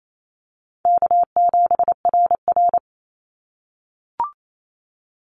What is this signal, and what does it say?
Morse code, heard as a keyed tone, about 28 wpm, 700 Hz. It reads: K7RR
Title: Call Signs - 28wpm